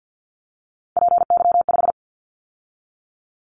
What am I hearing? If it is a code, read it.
Morse code, heard as a keyed tone, about 50 wpm, 700 Hz. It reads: PY5
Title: Callsign Prefixes for Top 50 DXCC - 50wpm